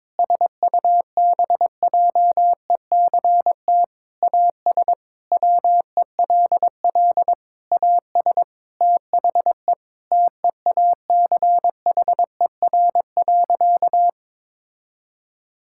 Morse code, heard as a keyed tone, about 22 wpm, 700 Hz. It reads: SUBJECT AS WELL AS THE TEACHER.
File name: practice / 22wpm / Sentences with 4-Word Phrases - 22wpm